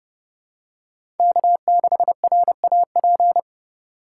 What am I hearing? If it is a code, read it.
Morse code, heard as a keyed tone, about 30 wpm, 700 Hz. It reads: K6RAP
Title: USA Call Signs - 5 Characters - 2x Character Spacing - 30wpm